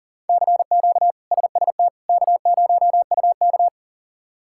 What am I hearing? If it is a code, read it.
Morse code, heard as a keyed tone, about 40 wpm, 700 Hz. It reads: CQ SST K0UK
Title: K1USN SST QSO Practice - 40wpm